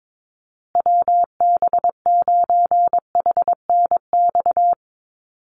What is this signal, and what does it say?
Morse code, heard as a keyed tone, about 22 wpm, 700 Hz. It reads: WB9HNX
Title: USA Call Signs Repeated 3x 22wpm